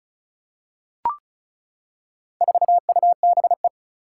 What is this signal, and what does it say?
Morse code, heard as a keyed tone, about 36 wpm, 700 Hz. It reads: 4UBE
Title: Four Challenging Letter-Number Combinations 35wpm